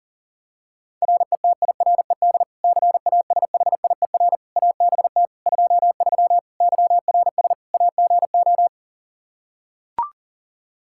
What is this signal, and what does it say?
Morse code, heard as a keyed tone, about 40 wpm, 700 Hz. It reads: RETIRED CASHIER ABT 23 YRS AGO
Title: Practice QSO - Exchange 3 - More Info - 40wpm